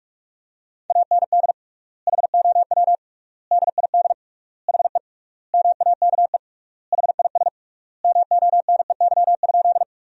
Morse code, heard as a keyed tone, about 46 wpm, 700 Hz. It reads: AND HOW DID HE MAKE HIS MONEY?
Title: 3x Word-Spacing - Sentences from Top 400 Words - 45wpm